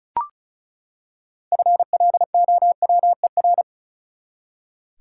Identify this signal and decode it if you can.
Morse code, heard as a keyed tone, about 34 wpm, 700 Hz. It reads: FLOWER
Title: Six Letter Words - 35wpm